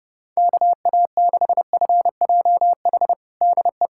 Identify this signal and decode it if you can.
Morse code, heard as a keyed tone, about 32 wpm, 700 Hz. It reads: KA6FJH DE
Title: USA Callsign DE Callsign - 6 Characters - 30wpm